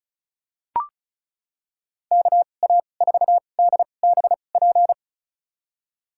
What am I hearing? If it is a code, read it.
Morse code, heard as a keyed tone, about 34 wpm, spaced as for 25 wpm, 700 Hz. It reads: KA4DBP
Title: USA Call Signs - 6 Characters - 2x Character Spacing - 35wpm